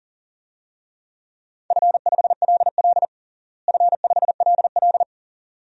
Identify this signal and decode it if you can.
Morse code, heard as a keyed tone, about 40 wpm, 700 Hz. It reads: F5LL F5LL
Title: DXCC-100 Call Signs - 4 Characters - Repeated 2x - 40wpm